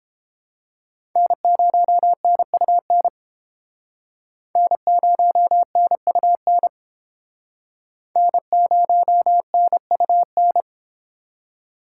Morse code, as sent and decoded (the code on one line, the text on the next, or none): -. ----- -. ..- -. / -. ----- -. ..- -. / -. ----- -. ..- -.
N0NUN N0NUN N0NUN